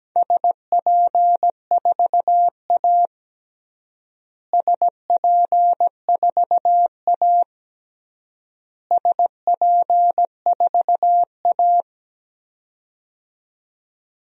Morse code, heard as a keyed tone, about 17 wpm, 700 Hz. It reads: SP4A SP4A SP4A
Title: DXCC-100 Call Signs - 4 Characters - Repeated 3x - 17wpm